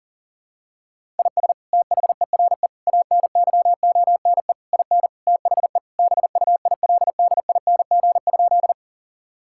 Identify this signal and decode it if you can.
Morse code, heard as a keyed tone, about 37 wpm, 700 Hz. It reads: IS THERE ANYONE IN THE BUILDING?
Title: Sentences from Top 1000 Words - Review Part 5 - 40wpm